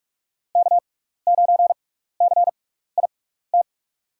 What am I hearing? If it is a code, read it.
Morse code, heard as a keyed tone, about 48 wpm, spaced as for 16 wpm, 700 Hz. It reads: K9CIT
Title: USA Call Signs - 6x Character Spacing - 45wpm